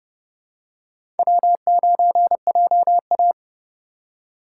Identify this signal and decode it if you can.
Morse code, heard as a keyed tone, about 30 wpm, 700 Hz. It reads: W9JA
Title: USA Call Signs - 3x Character Spacing - 30wpm